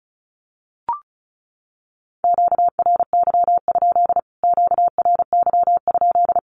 Morse code, heard as a keyed tone, about 34 wpm, 700 Hz. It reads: QRY? QRY?
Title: Q-Codes Repeated 3x 35wpm